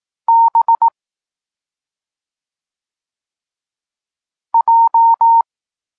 Morse code, heard as a keyed tone, about 18 wpm, 920 Hz. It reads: B J